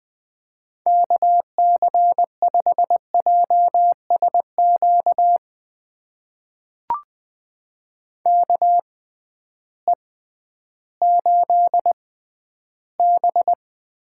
Morse code, heard as a keyed tone, about 20 wpm, spaced as for 12 wpm, 700 Hz. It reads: KC5JSQ KE8B
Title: USA Call Signs - 6 Characters - 6x Character Spacing - 20wpm